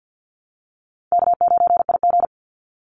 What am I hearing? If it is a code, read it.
Morse code, heard as a keyed tone, about 50 wpm, 700 Hz. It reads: K9IG
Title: Call Signs Repeated 2x - 50wpm